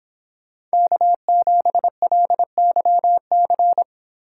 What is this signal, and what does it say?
Morse code, heard as a keyed tone, about 26 wpm, 700 Hz. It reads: K7LYC